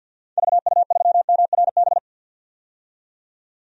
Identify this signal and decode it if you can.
Morse code, heard as a keyed tone, about 47 wpm, 700 Hz. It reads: VU3MRB
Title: DXCC-100 Call Signs - 6 Characters - 50wpm